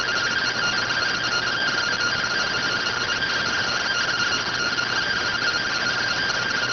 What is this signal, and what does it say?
Two-tone teleprinter signal, about 170 Hz shift.